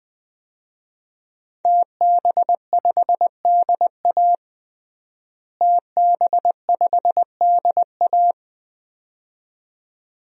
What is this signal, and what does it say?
Morse code, heard as a keyed tone, about 20 wpm, 700 Hz. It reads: TB5DA TB5DA
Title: DXCC-100 Call Signs - 5 Characters - Repeated 2x - 20wpm